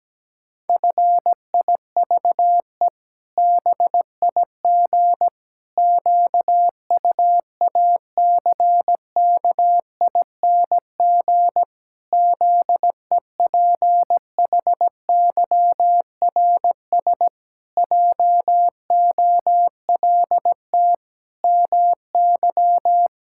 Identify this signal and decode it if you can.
Morse code, heard as a keyed tone, about 17 wpm, 700 Hz. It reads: FIVE BIG QUACKING ZEPHYRS JOLT MY
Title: Pangram Challenge v2 - 17wpm